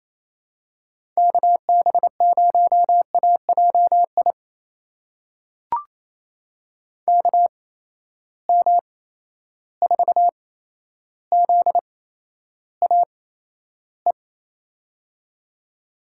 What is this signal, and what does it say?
Morse code, heard as a keyed tone, about 28 wpm, spaced as for 12 wpm, 700 Hz. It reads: KB0AJI KM4ZAE
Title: USA Call Signs - 6 Characters - 8x Character Spacing - 28wpm